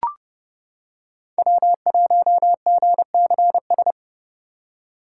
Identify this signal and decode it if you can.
Morse code, heard as a keyed tone, about 30 wpm, 700 Hz. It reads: W1GCS
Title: K1USN SST QSO Practice - 30wpm